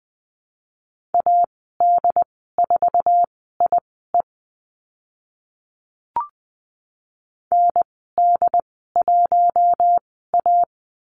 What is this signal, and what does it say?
Morse code, heard as a keyed tone, about 20 wpm, spaced as for 14 wpm, 700 Hz. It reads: AD4IE ND1A
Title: 069 Lesson 12g - Letter d - Call Signs 20wpm